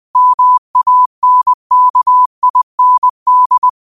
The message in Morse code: MANKIND

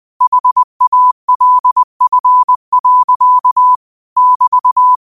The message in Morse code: HALF. <BT>

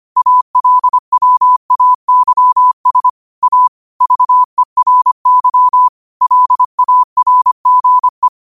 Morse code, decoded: ALWAYS A VERY LARGE

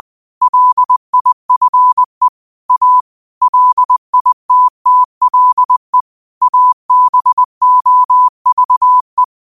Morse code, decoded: LIFE A LITTLE ABOVE

